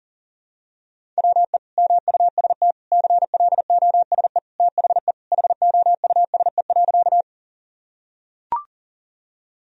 Morse code, heard as a keyed tone, about 40 wpm, 700 Hz. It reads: WE MUST CLOSE THE HOUSE.
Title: Sentences from Top 1000 Words - Review Part 1 - 40wpm